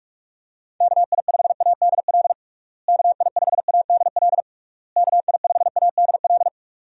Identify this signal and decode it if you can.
Morse code, heard as a keyed tone, about 46 wpm, 700 Hz. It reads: KI5ADL KI5ADL KI5ADL
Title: USA Call Signs - 6 Characters - Repeated 3x - 45wpm